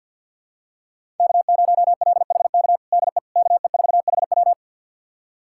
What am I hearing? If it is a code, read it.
Morse code, heard as a keyed tone, about 52 wpm, 700 Hz. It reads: K0LHK DE KE4HW
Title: USA Callsign DE Callsign - 5 Characters - 50wpm